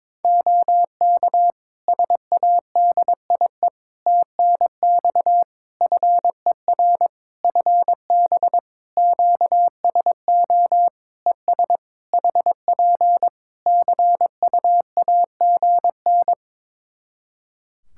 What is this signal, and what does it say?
Morse code, heard as a keyed tone, about 22 wpm, 700 Hz. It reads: OK SADIE TNX FER FB QSO ES HP CUAGN
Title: Practice QSO - Exchange 4 - Ending - 22wpm